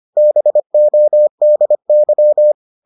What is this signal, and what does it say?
Morse code, heard as a keyed tone, about 25 wpm, 600 Hz. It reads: BODY